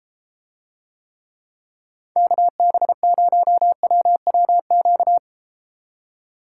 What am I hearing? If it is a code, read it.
Morse code, heard as a keyed tone, about 34 wpm, 700 Hz. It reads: KB0WWQ